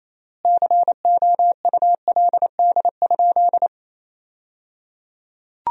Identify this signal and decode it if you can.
Morse code, heard as a keyed tone, about 28 wpm, 700 Hz. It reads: COULD?
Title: 3x Word-Spacing - Sentences from Top 400 Words - 28wpm